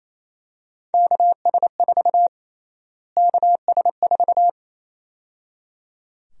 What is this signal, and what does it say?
Morse code, heard as a keyed tone, about 28 wpm, 700 Hz. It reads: KS4 KS4
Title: USA Call Sign Prefixes - Repeated 2x - 28wpm